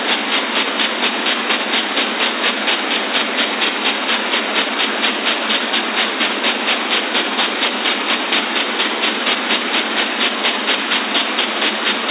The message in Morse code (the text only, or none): none